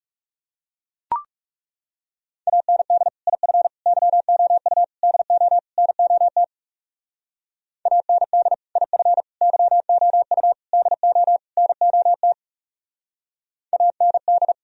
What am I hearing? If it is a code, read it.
Morse code, heard as a keyed tone, about 39 wpm, 700 Hz. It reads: AND IF YOU DO NOT AND IF YOU DO NOT AND